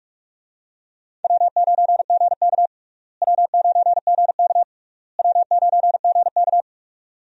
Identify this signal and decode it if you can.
Morse code, heard as a keyed tone, about 46 wpm, 700 Hz. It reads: W9GK W9GK W9GK
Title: USA Call Signs - 4 Characters - Repeated 3x - 45wpm